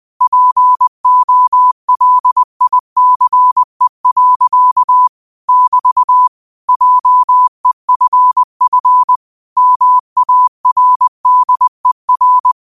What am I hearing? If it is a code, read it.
Morse code, heard as a keyed tone, about 20 wpm, 1000 Hz. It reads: POLICE. <BT> JEFF MARDER